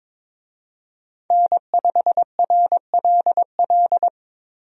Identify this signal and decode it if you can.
Morse code, heard as a keyed tone, about 21 wpm, 700 Hz. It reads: N5RLL
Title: USA Call Signs - 5 Characters - Repeated 2x - 22wpm